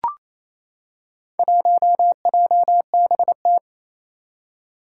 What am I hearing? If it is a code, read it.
Morse code, heard as a keyed tone, about 29 wpm, 700 Hz. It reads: 1JBT
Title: Four Challenging Letter-Number Combinations 28wpm